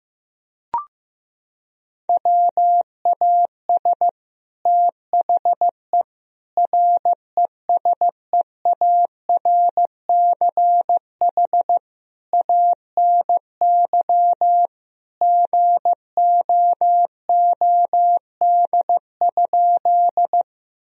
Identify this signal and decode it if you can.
Morse code, heard as a keyed tone, about 15 wpm, 700 Hz. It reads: WAS THE RESEARCH ANY GOOD?
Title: Sentences from Top 1000 Words - Review Part 4 - 15wpm